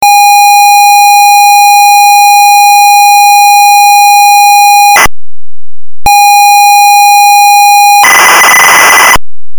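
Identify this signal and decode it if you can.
ZX Spectrum tape loading.
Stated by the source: ZX Spectrum tape loading the game Saboteur 2.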